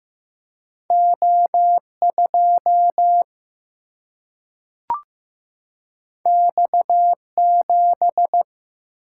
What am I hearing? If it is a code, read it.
Morse code, heard as a keyed tone, about 15 wpm, 700 Hz. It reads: O 2 X 7
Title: Single-Letter Single-Number 15wpm